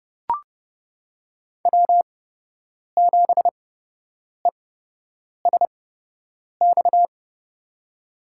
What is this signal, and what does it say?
Morse code, heard as a keyed tone, about 30 wpm, spaced as for 8 wpm, 700 Hz. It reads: W7ESX
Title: USA Call Signs - 8x Character Spacing - 30wpm